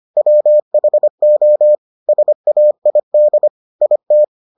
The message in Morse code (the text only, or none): WHO SAID IT